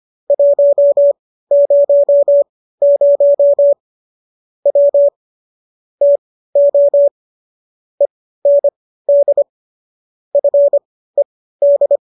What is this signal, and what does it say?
Morse code, heard as a keyed tone, about 25 wpm, spaced as for 15 wpm, 570 Hz. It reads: 100 W TO END FED